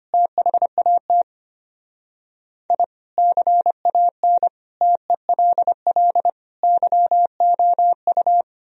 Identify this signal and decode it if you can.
Morse code, heard as a keyed tone, about 33 wpm, 700 Hz. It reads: THAT I CAN TELL YOU